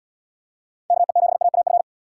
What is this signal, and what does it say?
Morse code, heard as a keyed tone, about 66 wpm, 700 Hz. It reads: KE7AAF